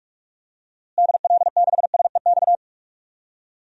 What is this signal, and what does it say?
Morse code, heard as a keyed tone, about 46 wpm, 700 Hz. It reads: DL6SEX
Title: DXCC-100 Call Signs - 6 Characters - Repeated 2x - 45wpm